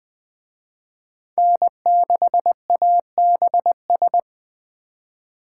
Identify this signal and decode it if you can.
Morse code, heard as a keyed tone, about 20 wpm, 700 Hz. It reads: N6ABS
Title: Speed Racing - USA 1x3 Call Signs 17wpm